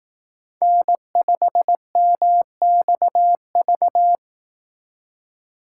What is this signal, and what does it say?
Morse code, heard as a keyed tone, about 18 wpm, 700 Hz. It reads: N5MXV
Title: Speed Racing - USA 1x3 Call Signs 15wpm